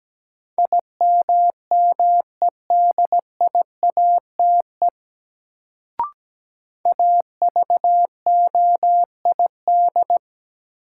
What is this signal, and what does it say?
Morse code, heard as a keyed tone, about 17 wpm, 700 Hz. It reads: IMMEDIATE AVOID
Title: Sentences from Top 1200 Words - New Word Drill Alt - 17wpm